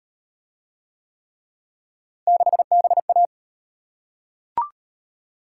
Morse code, heard as a keyed tone, about 34 wpm, 700 Hz. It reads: 6BA